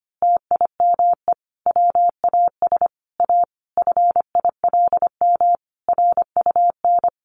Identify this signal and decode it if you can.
Morse code, heard as a keyed tone, about 25 wpm, 700 Hz. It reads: TIME WAS A FILM RUN
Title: A Sound Of Thunder 25wpm